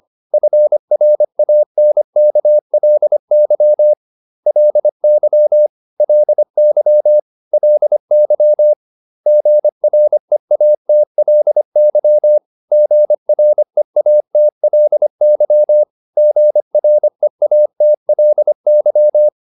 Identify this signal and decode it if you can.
Morse code, heard as a keyed tone, about 25 wpm, 600 Hz. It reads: FRANKLY LY LY LY GREATLY GREATLY GREATLY